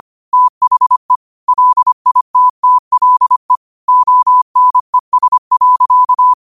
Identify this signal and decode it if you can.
Morse code, heard as a keyed tone, about 25 wpm, 1000 Hz. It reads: THE LITTLE ONES.